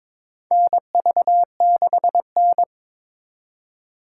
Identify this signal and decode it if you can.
Morse code, heard as a keyed tone, about 22 wpm, 700 Hz. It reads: NV6N
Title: USA Call Signs - 4x Character Spacing - 22wpm